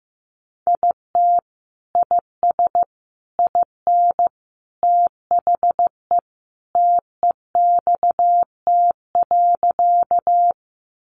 Morse code, heard as a keyed tone, about 15 wpm, 700 Hz. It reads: IT IS IN THE TEXT.
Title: Sentences from Top 1200 Words 15wpm